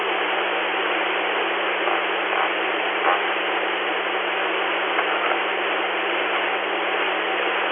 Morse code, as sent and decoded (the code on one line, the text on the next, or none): none